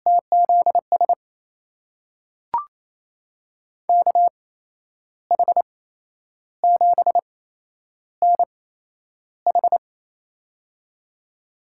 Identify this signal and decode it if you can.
Morse code, heard as a keyed tone, about 28 wpm, spaced as for 10 wpm, 700 Hz. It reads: TZS KH7NH
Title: USA Call Signs - 5 Characters - 8x Character Spacing - 28wpm